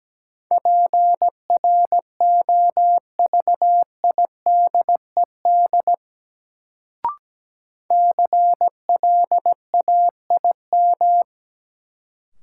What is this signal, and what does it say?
Morse code, heard as a keyed tone, about 17 wpm, 700 Hz. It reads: PROVIDED CLAIM
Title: Sentences from Top 1500 Words - New Word Drill - 17wpm